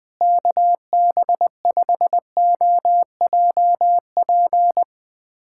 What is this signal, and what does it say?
Morse code, heard as a keyed tone, about 20 wpm, 700 Hz. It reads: KB5OJP
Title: USA Callsign DE Callsign 20wpm